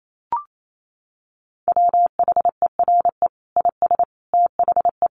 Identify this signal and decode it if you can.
Morse code, heard as a keyed tone, about 28 wpm, 700 Hz. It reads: WHERE IS THE
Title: Sentences from Top 1300 Words 28wpm